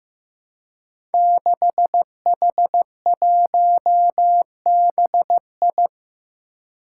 Morse code, heard as a keyed tone, about 15 wpm, 700 Hz. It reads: 6H1BI
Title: Five Challenging Letter-Number Combinations 15wpm